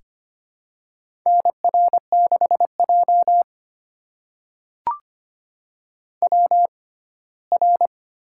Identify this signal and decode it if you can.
Morse code, heard as a keyed tone, about 25 wpm, spaced as for 17 wpm, 700 Hz. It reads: NR6J WR
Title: USA Call Signs - 4 Characters - 6x Character Spacing - 25wpm